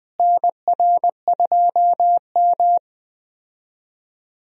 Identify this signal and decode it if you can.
Morse code, heard as a keyed tone, about 21 wpm, 700 Hz. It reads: NR2M
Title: USA Call Signs - 4 Characters - Repeated 3x - 20wpm